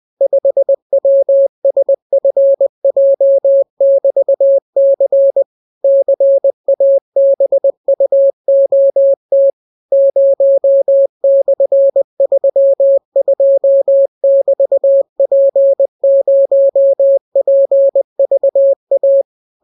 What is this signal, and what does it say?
Morse code, heard as a keyed tone, about 20 wpm, 550 Hz. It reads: 5WSFJ<BT>C CABUOT 0/32<BT>P0PVA